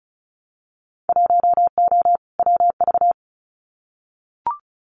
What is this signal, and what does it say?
Morse code, heard as a keyed tone, about 36 wpm, 700 Hz. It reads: 1O WV
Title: Winter Field Day Exchanges 35wpm